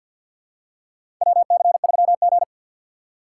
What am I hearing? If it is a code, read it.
Morse code, heard as a keyed tone, about 52 wpm, 700 Hz. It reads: WC3G
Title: USA Call Signs - 4 Characters - 2x Character Spacing - 50wpm